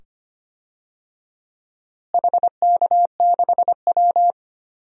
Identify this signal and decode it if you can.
Morse code, heard as a keyed tone, about 25 wpm, 700 Hz. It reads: HK6W
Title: DXCC-100 Call Signs - 4 Characters - Repeated 2x - 25wpm